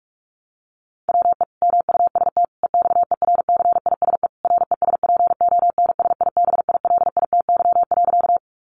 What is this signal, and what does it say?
Morse code, heard as a keyed tone, about 44 wpm, 700 Hz. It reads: WE MUST EXERCISE RESPONSIBILITY.
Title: Sentences from Top 1200 Words 45wpm